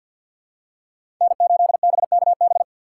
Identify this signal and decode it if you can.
Morse code, heard as a keyed tone, about 48 wpm, 700 Hz. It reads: N8BKB
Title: USA Call Signs - 5 Characters - 2x Character Spacing - 50wpm